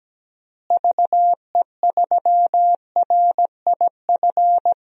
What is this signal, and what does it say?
Morse code, heard as a keyed tone, about 17 wpm, 700 Hz. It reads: VE3RIF
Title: DXCC-100 Call Signs - 6 Characters - Repeated 3x - 17wpm